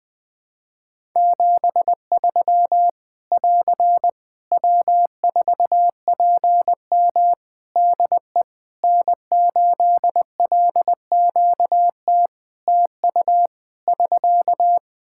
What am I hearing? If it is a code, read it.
Morse code, heard as a keyed tone, about 21 wpm, 700 Hz. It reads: 73 <AR> W4PM DE N8LQT TU <SK>
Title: Practice QSO - Exchange 4 - Ending - 20wpm